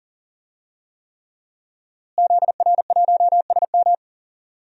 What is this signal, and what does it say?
Morse code, heard as a keyed tone, about 38 wpm, 700 Hz. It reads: ZR1SM
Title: DXCC-100 Call Signs - 5 Characters - Repeated 3x - 40wpm